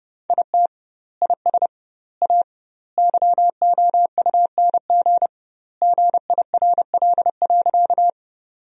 Morse code, heard as a keyed tone, about 30 wpm, 700 Hz. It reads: IT IS A YOUNG GIRL.
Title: 2x Word-Spacing - Sentences from Top 400 Words - 30wpm